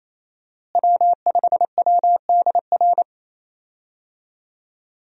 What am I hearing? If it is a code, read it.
Morse code, heard as a keyed tone, about 28 wpm, 700 Hz. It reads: W5WDR
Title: USA Call Signs - 5 Characters - Repeated 2x - 28wpm